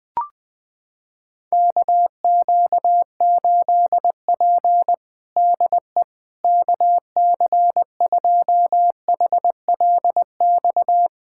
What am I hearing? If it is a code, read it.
Morse code, heard as a keyed tone, about 20 wpm, 700 Hz. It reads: KQ8P DE KC2HLX
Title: USA Callsign DE Callsign 20wpm